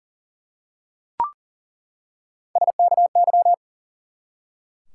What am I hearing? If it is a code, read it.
Morse code, heard as a keyed tone, about 38 wpm, 700 Hz. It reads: SKY
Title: Three Letter Words - 40wpm